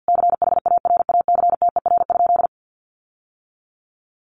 Morse code, heard as a keyed tone, about 50 wpm, 700 Hz. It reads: CHARACTER?
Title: Sentences from Top 1200 Words 50wpm